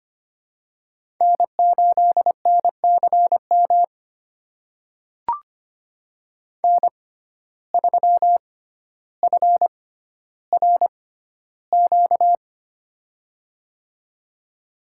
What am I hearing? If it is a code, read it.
Morse code, heard as a keyed tone, about 25 wpm, spaced as for 13 wpm, 700 Hz. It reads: N8NCM N3FRQ
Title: USA Call Signs - 5 Characters - 6x Character Spacing - 25wpm